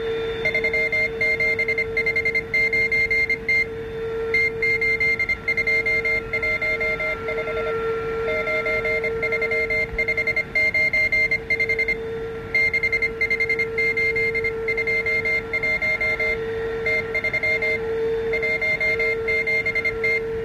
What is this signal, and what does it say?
Morse code, heard as a keyed tone, about 25 wpm, 2100 Hz. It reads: 3759T T8215 93595 65821 T317T